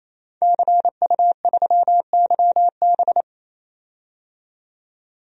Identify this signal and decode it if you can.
Morse code, heard as a keyed tone, about 28 wpm, 700 Hz. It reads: CU3YB
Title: DXCC-100 Call Signs - 5 Characters - 28wpm